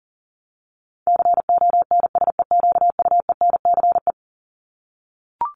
Morse code, heard as a keyed tone, about 40 wpm, 700 Hz. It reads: CONSEQUENCE
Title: LY Suffix - Top 100 Words with Base Words 40wpm